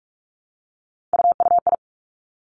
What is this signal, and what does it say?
Morse code, heard as a keyed tone, about 46 wpm, 700 Hz. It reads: UUI
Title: Three Challenging Letter-Number Combinations 45wpm